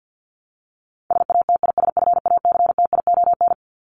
Morse code, heard as a keyed tone, about 50 wpm, 700 Hz. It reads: SATISFACTION
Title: Top 1751-2000 Words 50wpm